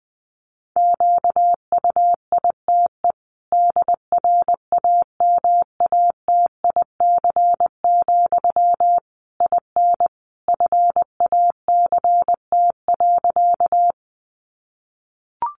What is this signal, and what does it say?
Morse code, heard as a keyed tone, about 20 wpm, 700 Hz. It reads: QUITE DRAMATIC, IN FACT.
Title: Sentences from Top 1300 Words 20wpm